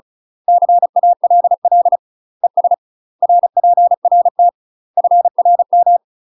Morse code, heard as a keyed tone, about 36 wpm, 700 Hz. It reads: CALL ES RPRT FRM
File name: UK MASTER QSOs TRAINING / 35WPM / UK MASTER QSOs 0033